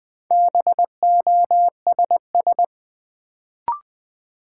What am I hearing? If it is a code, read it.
Morse code, heard as a keyed tone, about 20 wpm, 700 Hz. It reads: BOSS
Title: Sentences from Top 700 Words - New Word Drill - 20wpm